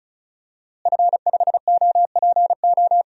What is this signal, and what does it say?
Morse code, heard as a keyed tone, about 36 wpm, 700 Hz. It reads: F5OPO
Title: DXCC-100 Call Signs - 5 Characters - Repeated 3x - 35wpm